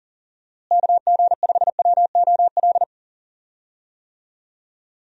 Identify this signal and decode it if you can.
Morse code, heard as a keyed tone, about 40 wpm, 700 Hz. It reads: KG5WOL
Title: USA Call Signs - 6 Characters - Repeated 3x - 40wpm